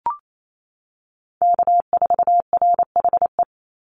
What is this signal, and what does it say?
Morse code, heard as a keyed tone, about 28 wpm, 700 Hz. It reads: K4RHE
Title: USA Call Signs 28wpm